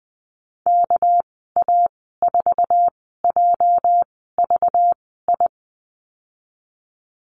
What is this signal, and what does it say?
Morse code, heard as a keyed tone, about 19 wpm, spaced as for 14 wpm, 700 Hz. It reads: KA4JVI
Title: 207 Lesson 33g - Letter j - Call Signs 20wpm